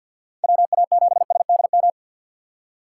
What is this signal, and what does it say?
Morse code, heard as a keyed tone, about 52 wpm, 700 Hz. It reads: WA7SDM
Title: USA Call Signs - 6 Characters - 4x Character Spacing - 50wpm